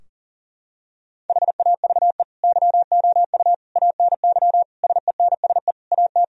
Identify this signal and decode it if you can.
Morse code, heard as a keyed tone, about 37 wpm, 700 Hz. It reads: HAVE YOU ANY SENSE AT
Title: Sentences from Top 1000 Words - Review Part 4 - 40wpm